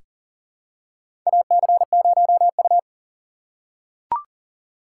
Morse code, heard as a keyed tone, about 38 wpm, 690 Hz. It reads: AC0U
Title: USA Call Signs - 4 Characters - 2x Character Spacing - 40wpm